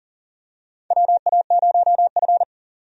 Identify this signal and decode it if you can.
Morse code, heard as a keyed tone, about 40 wpm, 700 Hz. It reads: WA0F
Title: USA Call Signs - 4 Characters - 40wpm